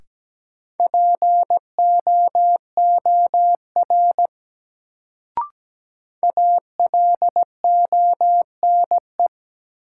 Morse code, heard as a keyed tone, about 17 wpm, 700 Hz. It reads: POOR ALONE
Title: Sentences from Top 1000 Words - New Word Drill - 17wpm